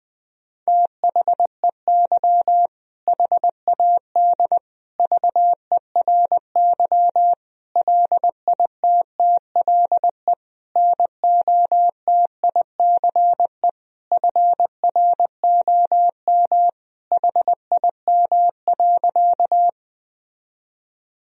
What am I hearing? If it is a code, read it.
Morse code, heard as a keyed tone, about 20 wpm, 700 Hz. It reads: THEY HAD VERY LITTLE NOTICE FROM HIM.
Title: Sentences from Top 1000 Words - Review Part 2 - 20wpm